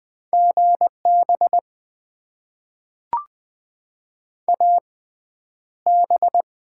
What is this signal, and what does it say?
Morse code, heard as a keyed tone, about 20 wpm, spaced as for 10 wpm, 700 Hz. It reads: GB AB
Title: Top 90 QSO Elements - 6x Character Spacing - 20wpm